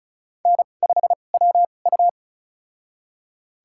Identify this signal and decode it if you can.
Morse code, heard as a keyed tone, about 34 wpm, spaced as for 25 wpm, 700 Hz. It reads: N5WU
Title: USA Call Signs - 4 Characters - 2x Character Spacing - 35wpm